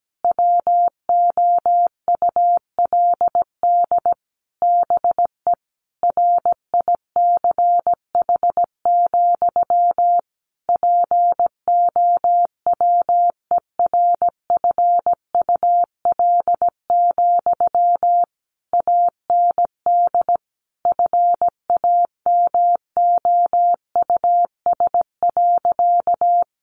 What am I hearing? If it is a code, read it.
Morse code, heard as a keyed tone, about 17 wpm, 700 Hz. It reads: WOULD BE RICH, POWERFUL, AND FAMOUS.
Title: Sentences from Top 1500 Words 17wpm